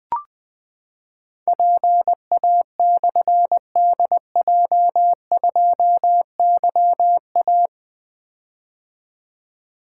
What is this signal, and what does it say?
Morse code, heard as a keyed tone, about 20 wpm, 700 Hz. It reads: PA/DJ2YA
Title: Difficult Call Signs 20wpm